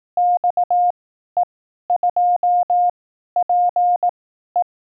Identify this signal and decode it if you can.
Morse code, heard as a keyed tone, about 18 wpm, spaced as for 12 wpm, 710 Hz. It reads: XE2PE
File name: 12WPM test